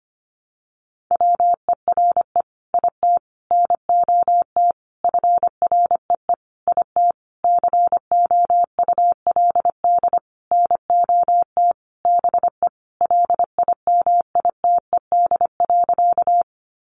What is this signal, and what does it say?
Morse code, heard as a keyed tone, about 25 wpm, 700 Hz. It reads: WERE IT NOT FREE IT COULD NOT BE LIMITED.
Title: Sentences from Top 900 Words 25wpm